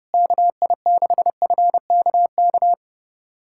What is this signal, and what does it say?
Morse code, heard as a keyed tone, about 30 wpm, 700 Hz. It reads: KI6FKK
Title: USA Callsign DE Callsign - 6 Characters - 30wpm